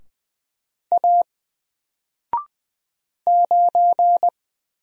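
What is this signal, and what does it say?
Morse code, heard as a keyed tone, about 20 wpm, 700 Hz. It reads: A 9
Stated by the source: Single letter and number Morse drill at 20 wpm, from N4LCD.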